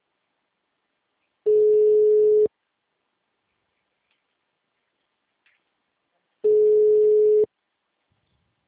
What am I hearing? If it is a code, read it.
Ringback tone (Europe, 425 Hz).